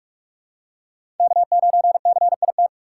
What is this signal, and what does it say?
Morse code, heard as a keyed tone, about 46 wpm, 700 Hz. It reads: K9CIT
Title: USA Call Signs - 5 Characters - 2x Character Spacing - 45wpm